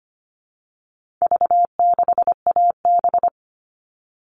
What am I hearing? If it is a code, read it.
Morse code, heard as a keyed tone, about 25 wpm, 700 Hz. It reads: V6AB
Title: All Cut Numbers - 4-Digit 25wpm